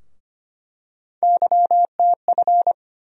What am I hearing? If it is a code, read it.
Morse code, heard as a keyed tone, about 25 wpm, 700 Hz. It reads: YTF
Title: USA Call Sign Postfixes - 3 Letters 25wpm